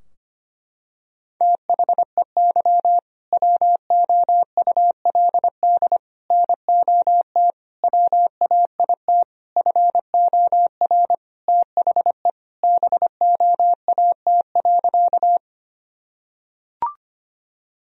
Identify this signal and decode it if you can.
Morse code, heard as a keyed tone, about 25 wpm, 700 Hz. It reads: THEY WOULD NOT WAIT FOR THE BOAT.
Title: Sentences from Top 1000 Words - Review Part 3 - 25wpm